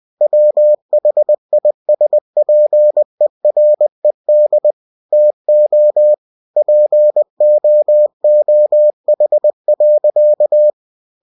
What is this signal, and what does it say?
Morse code, heard as a keyed tone, about 20 wpm, 600 Hz. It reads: WHISPERED TO POOH.
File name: cw-books / winnie-the-pooh / winnie-20wpm-page-0108